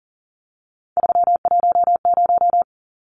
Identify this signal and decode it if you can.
Morse code, heard as a keyed tone, about 40 wpm, 700 Hz. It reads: 310
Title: Numbers - 3-Digit 40wpm